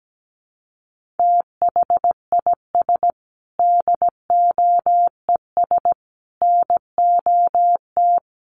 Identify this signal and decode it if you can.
Morse code, heard as a keyed tone, about 17 wpm, 700 Hz. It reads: THIS DOES NOT
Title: Sets of 5 Words Encore2 17wpm